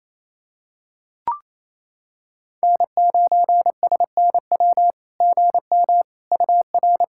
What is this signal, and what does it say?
Morse code, heard as a keyed tone, about 28 wpm, 700 Hz. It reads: N9SNW GM UR
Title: SOTA QSO Practice 28wpm